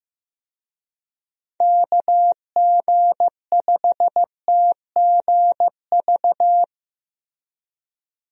Morse code, heard as a keyed tone, about 15 wpm, 700 Hz. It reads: KG5TGV
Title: USA Call Signs - 6 Characters - Repeated 3x - 15wpm